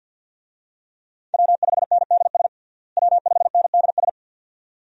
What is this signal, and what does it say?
Morse code, heard as a keyed tone, about 53 wpm, 700 Hz. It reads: W5NDS W5NDS
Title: USA Call Signs - 5 Characters - Repeated 2x - 50wpm